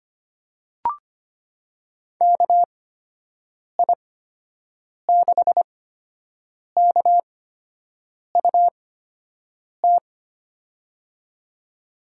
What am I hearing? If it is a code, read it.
Morse code, heard as a keyed tone, about 25 wpm, spaced as for 7 wpm, 700 Hz. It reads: KI6KUT
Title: USA Call Signs - 8x Character Spacing - 25wpm